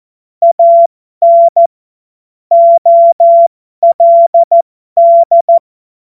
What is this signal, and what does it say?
Morse code, heard as a keyed tone, about 14 wpm, spaced as for 12 wpm, 680 Hz. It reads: AN OLD
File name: Morsekurs / Witze